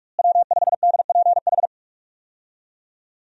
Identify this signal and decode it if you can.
Morse code, heard as a keyed tone, about 41 wpm, 700 Hz. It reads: W5DPH
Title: USA Call Signs - 5 Characters - Repeated 2x - 45wpm